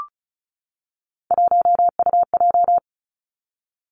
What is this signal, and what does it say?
Morse code, heard as a keyed tone, about 36 wpm, 700 Hz. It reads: 1UJ
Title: Three Challenging Letter-Number Combinations 35wpm